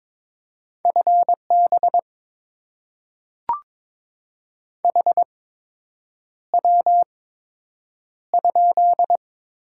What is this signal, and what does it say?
Morse code, heard as a keyed tone, about 22 wpm, spaced as for 8 wpm, 700 Hz. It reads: FB HW?
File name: Top 20 QSO Elements - 8x Character Spacing - 22wpm